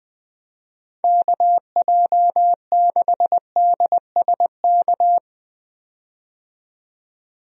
Morse code, heard as a keyed tone, about 20 wpm, 700 Hz. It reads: KJ6DSK
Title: USA Call Signs - 6 Characters - 20wpm